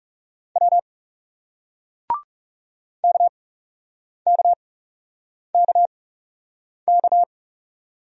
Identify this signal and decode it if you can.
Morse code, heard as a keyed tone, about 40 wpm, spaced as for 8 wpm, 700 Hz. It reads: W KKKK